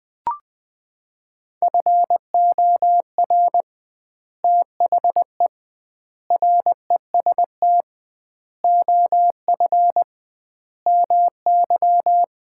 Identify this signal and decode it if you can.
Morse code, heard as a keyed tone, about 20 wpm, 700 Hz. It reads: FOR THE REST OF MY
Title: 2x Word-Spacing - Sets of 5 Words - 20wpm